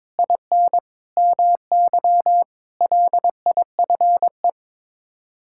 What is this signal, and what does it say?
Morse code, heard as a keyed tone, about 22 wpm, 700 Hz. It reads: IN MY LIFE
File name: practice / 22wpm / Sets of 7 Words - 22wpm